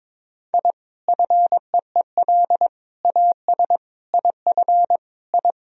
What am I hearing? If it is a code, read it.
Morse code, heard as a keyed tone, about 22 wpm, 700 Hz. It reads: I FEEL AS IF I
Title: Sentences from Top 1000 Words - Review Part 3 - 22wpm